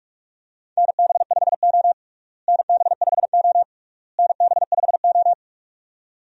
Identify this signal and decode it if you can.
Morse code, heard as a keyed tone, about 41 wpm, 700 Hz. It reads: NB5O NB5O NB5O
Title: USA Call Signs - 4 Characters - Repeated 3x - 45wpm